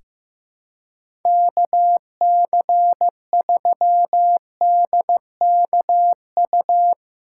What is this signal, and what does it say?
Morse code, heard as a keyed tone, about 15 wpm, 700 Hz. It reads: KC3DKU
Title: USA Callsign DE Callsign 15wpm